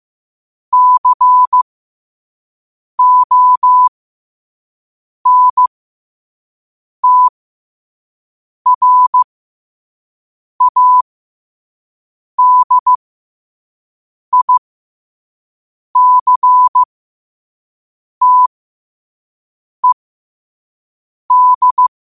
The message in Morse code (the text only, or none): CONTRADICTED